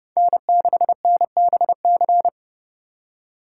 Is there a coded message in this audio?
Yes: Morse code from 0.2 to 2.3 s, heard as a keyed tone, about 29 wpm, 700 Hz. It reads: N6NBC